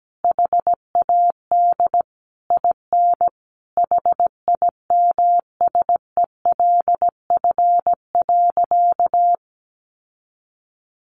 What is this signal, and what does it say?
Morse code, heard as a keyed tone, about 17 wpm, 700 Hz. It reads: HAD IN HIMSELF.
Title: Sentences from Top 1500 Words 17wpm